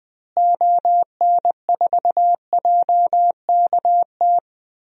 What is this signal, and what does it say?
Morse code, heard as a keyed tone, about 20 wpm, 700 Hz. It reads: ON4JKT
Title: DXCC-100 Call Signs - 6 Characters - Repeated 2x - 20wpm